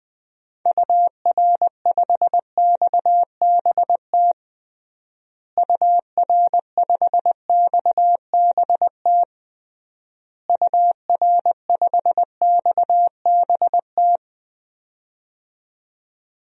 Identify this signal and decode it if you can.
Morse code, heard as a keyed tone, about 20 wpm, 700 Hz. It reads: UR5XBT UR5XBT UR5XBT
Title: DXCC-100 Call Signs - 6 Characters - Repeated 3x - 20wpm